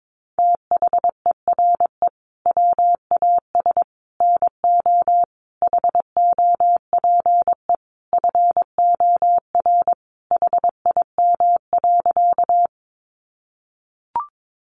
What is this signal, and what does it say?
Morse code, heard as a keyed tone, about 22 wpm, 700 Hz. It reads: THERE WAS NO HOPE FOR HIM.
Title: Sentences from Top 1000 Words 22wpm